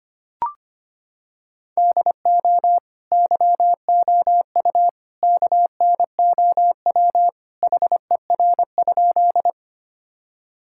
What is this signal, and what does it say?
Morse code, heard as a keyed tone, about 25 wpm, 700 Hz. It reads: DO YOU KNOW HER?
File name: practice / 25wpm / Sentences with 3-Word Phrases - 25wpm